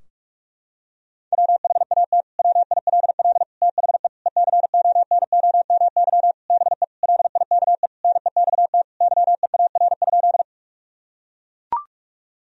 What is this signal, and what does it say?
Morse code, heard as a keyed tone, about 41 wpm, 700 Hz. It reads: WHAT WILL THE ECONOMY BE LIKE NEXT YEAR?
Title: Sentences from Top 1000 Words - Review Part 3 - 45wpm